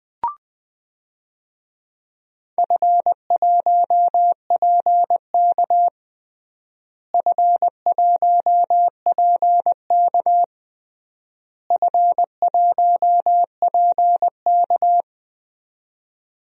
Morse code, heard as a keyed tone, about 20 wpm, 700 Hz. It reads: F1PK F1PK F1PK
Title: DXCC-100 Call Signs - 4 Characters - Repeated 3x - 20wpm